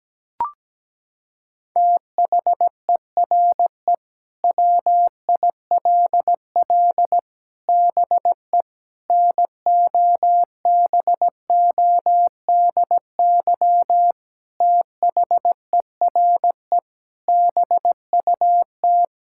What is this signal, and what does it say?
Morse code, heard as a keyed tone, about 17 wpm, 700 Hz. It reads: THERE WILL BE NOBODY THERE BUT
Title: Sentences from Top 1600 Words 17wpm